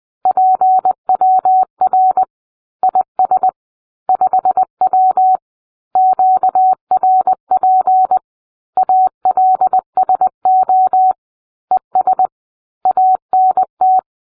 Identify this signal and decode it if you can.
Morse code, heard as a keyed tone, about 20 wpm, 750 Hz. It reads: PWR IS 5W QRP ALSO ES ANT